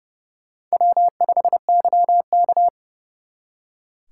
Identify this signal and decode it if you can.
Morse code, heard as a keyed tone, about 38 wpm, 700 Hz. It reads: W5YK